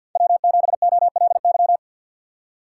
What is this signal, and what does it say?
Morse code, heard as a keyed tone, about 51 wpm, 700 Hz. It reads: W7OLY
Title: USA Call Signs - 4x Character Spacing - 50wpm